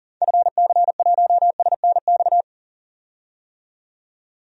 Morse code, heard as a keyed tone, about 37 wpm, 700 Hz. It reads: FC1SNX
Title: DXCC-100 Call Signs - 6 Characters - 40wpm